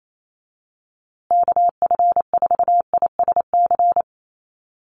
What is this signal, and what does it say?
Morse code, heard as a keyed tone, about 28 wpm, 700 Hz. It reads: KF4ISC
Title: USA Call Signs Repeated 3x 28wpm